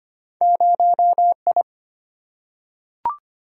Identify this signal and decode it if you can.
Morse code, heard as a keyed tone, about 25 wpm, 700 Hz. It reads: 0I
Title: Single-Letter Single-Number 25wpm